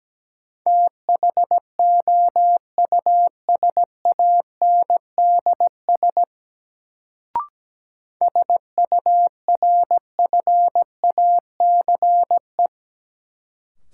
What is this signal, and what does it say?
Morse code, heard as a keyed tone, about 17 wpm, 700 Hz. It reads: THOUSANDS SURFACE
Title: Sets of 4 Words - New Word Drill - 17wpm